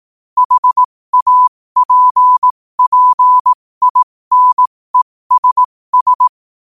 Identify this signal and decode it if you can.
Morse code, heard as a keyed tone, about 18 wpm, spaced as for 15 wpm, 1000 Hz. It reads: HAPPINESS
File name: qotdfiles / 15wpm